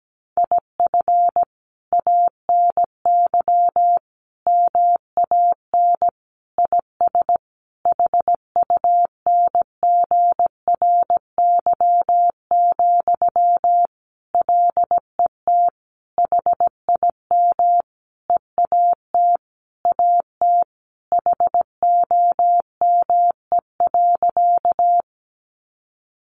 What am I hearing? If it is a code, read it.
Morse code, heard as a keyed tone, about 17 wpm, 700 Hz. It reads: IF ANY MAN IS HUNGRY, LET HIM EAT AT HOME.
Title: Sentences from Top 1400 Words 17wpm